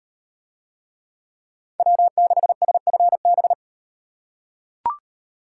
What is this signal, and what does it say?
Morse code, heard as a keyed tone, about 38 wpm, 700 Hz. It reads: W6SFB